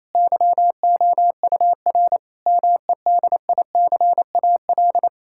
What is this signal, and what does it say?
Morse code, heard as a keyed tone, about 28 wpm, 700 Hz. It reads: YOUR MEDICAL
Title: Sentences from Top 1000 Words - Review Part 5 - 28wpm